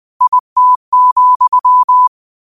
Morse code, heard as a keyed tone, about 20 wpm, 1000 Hz. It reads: IT,
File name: qotdfiles / 20wpm